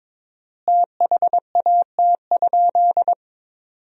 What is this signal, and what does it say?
Morse code, heard as a keyed tone, about 22 wpm, 700 Hz. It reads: THAT?
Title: 4x Word-Spacing - Sentences from Top 300 Words - 22wpm